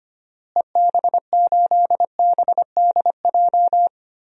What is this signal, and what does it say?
Morse code, heard as a keyed tone, about 25 wpm, 700 Hz. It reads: EB8BDJ
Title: DXCC-100 Call Signs - 6 Characters - Repeated 3x - 25wpm